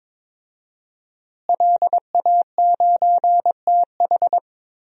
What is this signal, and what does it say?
Morse code, heard as a keyed tone, about 22 wpm, 700 Hz. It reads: LA9TH
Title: DXCC-100 Call Signs - 5 Characters - Repeated 3x - 22wpm